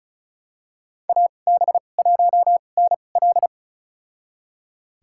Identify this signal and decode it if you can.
Morse code, heard as a keyed tone, about 36 wpm, spaced as for 25 wpm, 700 Hz. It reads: AB1NL
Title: USA Call Signs - 5 Characters - 2x Character Spacing - 35wpm